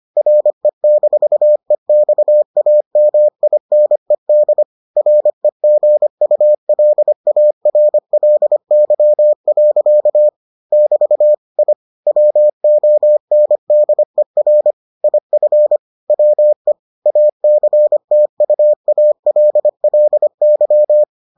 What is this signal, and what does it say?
Morse code, heard as a keyed tone, about 25 wpm, 600 Hz. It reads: RE-EXAMINED REGULARLY. <BT> I WONDER IF WE ACTUALLY